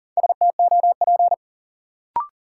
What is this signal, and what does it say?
Morse code, heard as a keyed tone, about 40 wpm, 700 Hz. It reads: STOP
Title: Sentences from Top 300 Words - New Word Drill - 40wpm